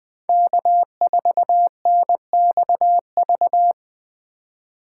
Morse code, heard as a keyed tone, about 20 wpm, 700 Hz. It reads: K4NXV
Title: Speed Racing - USA 1x3 Call Signs 17wpm